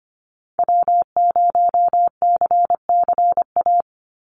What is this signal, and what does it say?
Morse code, heard as a keyed tone, about 25 wpm, 700 Hz. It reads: W0CCA
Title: NA SOTA Activator Callsigns 25wpm